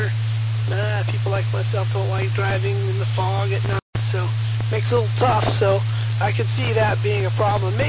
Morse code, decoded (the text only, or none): none